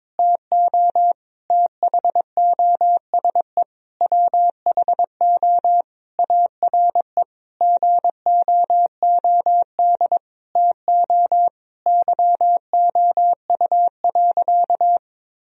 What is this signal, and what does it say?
Morse code, heard as a keyed tone, about 22 wpm, 700 Hz. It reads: TO THOSE WHO ARE GOOD TO YOU.
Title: Sentences from Top 1000 Words - Review Part 1 - 22wpm